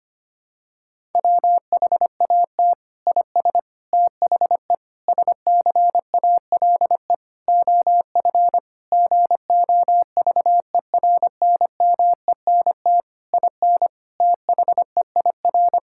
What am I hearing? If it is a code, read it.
Morse code, heard as a keyed tone, about 25 wpm, 700 Hz. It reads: WHAT IS THE SCALE OF GOVERNMENT IN THEIR
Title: Sentences from Top 1000 Words - Review Part 5 - 25wpm